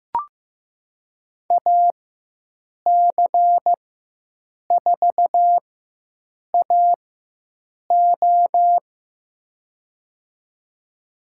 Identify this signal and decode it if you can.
Morse code, heard as a keyed tone, about 15 wpm, spaced as for 7 wpm, 700 Hz. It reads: AC4AO
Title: USA Call Signs - 5 Characters - 4x Character Spacing - 15wpm